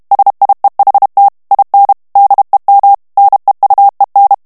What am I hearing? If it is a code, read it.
Morse code, heard as a keyed tone, about 32 wpm, 800 Hz. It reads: SIEHT IN DEM NEUEN
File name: ALLEMAND texte clair 1 vitesse 30 mots minute